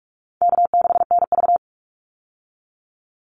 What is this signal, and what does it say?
Morse code, heard as a keyed tone, about 44 wpm, 700 Hz. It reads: K6NV
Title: Easy Call Signs - 45wpm